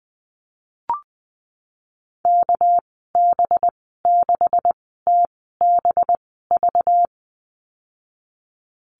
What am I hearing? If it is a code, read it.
Morse code, heard as a keyed tone, about 20 wpm, spaced as for 15 wpm, 700 Hz. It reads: KB6TBV
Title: 200 Lesson 32g - Letter k - Call Signs 20wpm